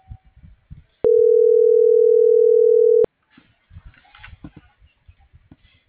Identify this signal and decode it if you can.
Ringback tone (North America), 440 + 480 Hz.